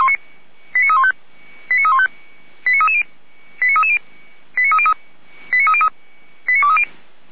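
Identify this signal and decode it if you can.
Five-tone selective call (ZVEI), heard as quick runs of five tones, tones of about 70 ms. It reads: 89315 89315 89330 89330 89303 89303
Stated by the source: ZVEI five-tone call received on an SDR.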